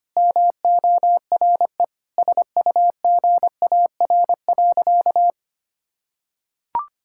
Morse code, heard as a keyed tone, about 25 wpm, 700 Hz. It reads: MORE SUGAR.
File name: practice / 25wpm / Sentences with 4-Word Phrases - 25wpm